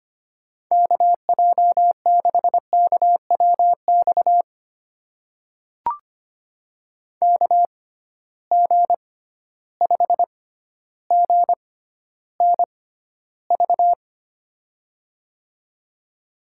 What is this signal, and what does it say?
Morse code, heard as a keyed tone, about 25 wpm, spaced as for 13 wpm, 700 Hz. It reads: KJ6KWX KG5GNV
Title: USA Call Signs - 6 Characters - 6x Character Spacing - 25wpm